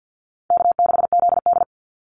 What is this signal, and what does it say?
Morse code, heard as a keyed tone, about 50 wpm, 700 Hz. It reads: K6ZD
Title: Easy Call Signs - 50wpm